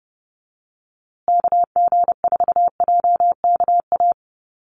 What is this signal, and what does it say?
Morse code, heard as a keyed tone, about 30 wpm, 700 Hz. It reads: KG4JKA
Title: USA Call Signs Repeated 2x 30wpm